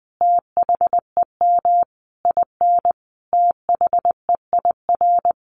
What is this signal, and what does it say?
Morse code, heard as a keyed tone, about 19 wpm, 700 Hz. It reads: THEM IN THEIR
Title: Sentences from Top 1100 Words 20wpm